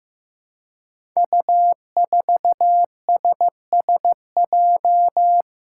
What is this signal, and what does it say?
Morse code, heard as a keyed tone, about 15 wpm, 700 Hz. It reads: U4SSJ
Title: Five Challenging Letter-Number Combinations 15wpm